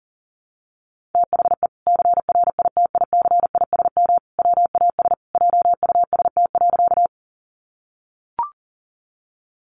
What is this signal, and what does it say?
Morse code, heard as a keyed tone, about 40 wpm, 700 Hz. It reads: THE CRITICISM WAS JUST.
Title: Sentences from Top 1300 Words 40wpm